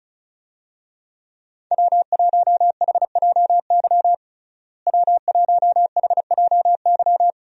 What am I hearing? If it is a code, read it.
Morse code, heard as a keyed tone, about 34 wpm, 700 Hz. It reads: W1HJY W1HJY
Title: USA Call Signs - 5 Characters - Repeated 2x - 35wpm